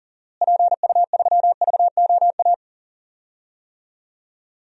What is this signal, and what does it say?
Morse code, heard as a keyed tone, about 40 wpm, 700 Hz. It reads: PU3VOA
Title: DXCC-100 Call Signs - 6 Characters - Repeated 3x - 40wpm